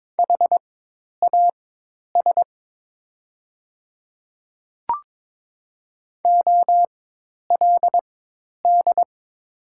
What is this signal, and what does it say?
Morse code, heard as a keyed tone, about 22 wpm, spaced as for 10 wpm, 700 Hz. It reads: HAS OLD
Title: 4x Character Spacing - Top 300 Words - 22wpm